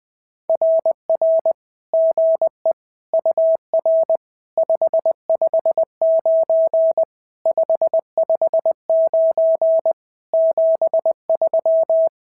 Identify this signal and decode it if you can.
Morse code, heard as a keyed tone, about 20 wpm, 650 Hz. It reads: RR GE UR 559 559 73
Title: SOTA QSO Practice 20wpm